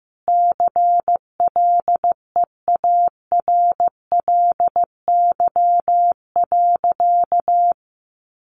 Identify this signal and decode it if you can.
Morse code, heard as a keyed tone, about 15 wpm, 700 Hz. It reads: CLEARLY.
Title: Sentences from Top 1300 Words 15wpm